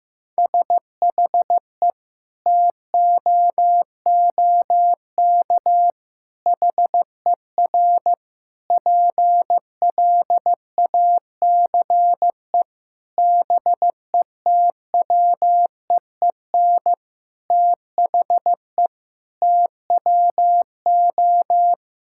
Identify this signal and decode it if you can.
Morse code, heard as a keyed tone, about 15 wpm, 700 Hz. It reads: SHE TOOK HER PLACE BETWEEN THE TWO
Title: Sentences from Top 1000 Words - Review Part 5 - 15wpm